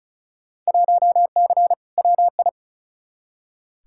Morse code, heard as a keyed tone, about 34 wpm, 700 Hz. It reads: 1C WI
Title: Field Day Exchanges - 35wpm